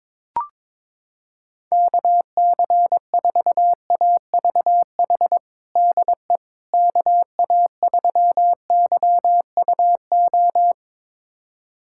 Morse code, heard as a keyed tone, about 22 wpm, 700 Hz. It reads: KC4AVH DE KA3YUO
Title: USA Callsign DE Callsign - 6 Characters - 22wpm